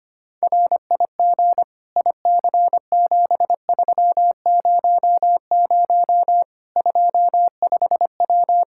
Morse code, heard as a keyed tone, about 25 wpm, 700 Hz. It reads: RIG IC7300 25W
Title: Practice QSO - Exchange 2 - Info - 25wpm